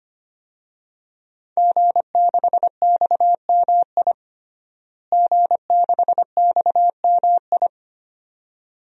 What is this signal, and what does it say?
Morse code, heard as a keyed tone, about 25 wpm, 700 Hz. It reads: G6XMI G6XMI
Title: DXCC-100 Call Signs - 5 Characters - Repeated 2x - 25wpm